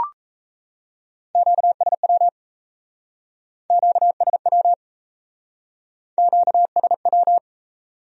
Morse code, heard as a keyed tone, about 36 wpm, 700 Hz. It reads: QSW QSW QSW